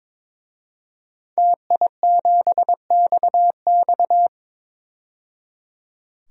Morse code, heard as a keyed tone, about 22 wpm, 700 Hz. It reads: TI7XX
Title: DXCC-100 Call Signs - 5 Characters - 22wpm